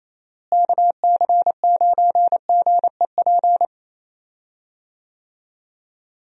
Morse code, heard as a keyed tone, about 28 wpm, 700 Hz. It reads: KC9GEP
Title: USA Call Signs - 6 Characters - Repeated 2x - 28wpm